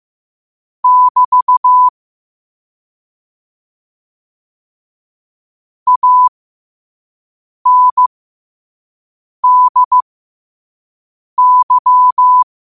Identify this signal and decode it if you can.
Morse code, heard as a keyed tone, about 15 wpm, spaced as for 5 wpm, 1000 Hz. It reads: <BT> ANDY
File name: qotdfiles / 5wpm